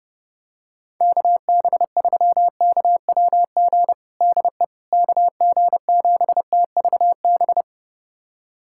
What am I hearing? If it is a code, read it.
Morse code, heard as a keyed tone, about 32 wpm, 700 Hz. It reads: KB3KWG DE KG7TVB
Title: USA Callsign DE Callsign 30wpm